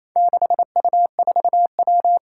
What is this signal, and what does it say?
Morse code, heard as a keyed tone, about 28 wpm, 700 Hz. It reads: 6U4W
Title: Four Challenging Letter-Number Combinations 28wpm